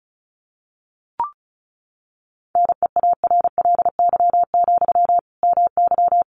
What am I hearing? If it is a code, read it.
Morse code, heard as a keyed tone, about 34 wpm, 700 Hz. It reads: NEARLY, MY
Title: Sentences from Top 1300 Words 35wpm